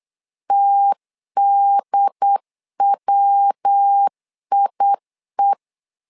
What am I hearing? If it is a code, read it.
Morse code, heard as a keyed tone, about 8 wpm, 800 Hz. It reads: TDWIE
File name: LEZIONI / 3 MWRO / 3 MW 8wpm 2501